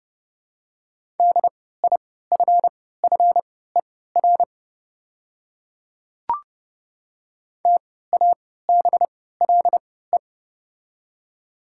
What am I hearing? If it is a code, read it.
Morse code, heard as a keyed tone, about 30 wpm, spaced as for 17 wpm, 700 Hz. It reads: DIFFER TABLE
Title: 3x Character Spacing - Top 500 Words - 30wpm